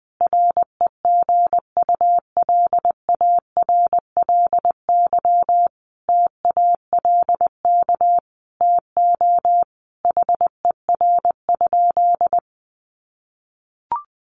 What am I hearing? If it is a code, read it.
Morse code, heard as a keyed tone, about 20 wpm, 700 Hz. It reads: REGULARLY TALK TO HER?
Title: Sentences from Top 1400 Words 20wpm